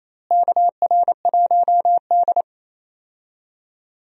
Morse code, heard as a keyed tone, about 27 wpm, 700 Hz. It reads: KR1D
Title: USA Call Signs - 4 Characters - Repeated 3x - 28wpm